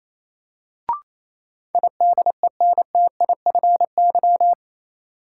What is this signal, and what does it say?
Morse code, heard as a keyed tone, about 28 wpm, 700 Hz. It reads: IDENTIFY
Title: Sentences from Top 900 Words - New Word Drill - 28wpm